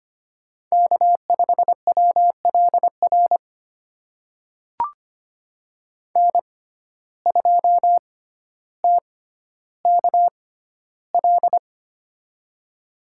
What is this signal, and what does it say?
Morse code, heard as a keyed tone, about 25 wpm, spaced as for 13 wpm, 700 Hz. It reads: K5WLR N2TKL
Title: USA Call Signs - 5 Characters - 6x Character Spacing - 25wpm